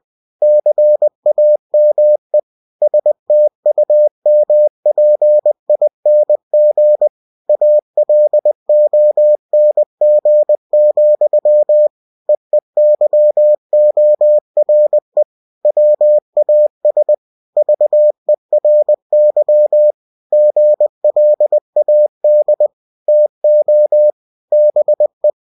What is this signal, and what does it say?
Morse code, heard as a keyed tone, about 20 wpm, 600 Hz. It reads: CAME STUMPING ALONG, EEYORE WAS VERY GLAD TO BE
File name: cw-books / winnie-the-pooh / winnie-20wpm-page-0026